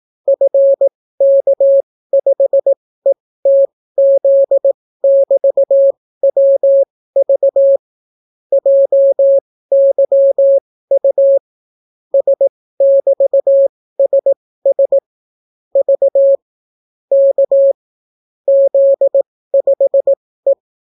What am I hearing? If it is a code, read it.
Morse code, heard as a keyed tone, about 18 wpm, spaced as for 14 wpm, 550 Hz. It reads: FK5ETZ<BT>WV JYU S<BT>SS V K Z5E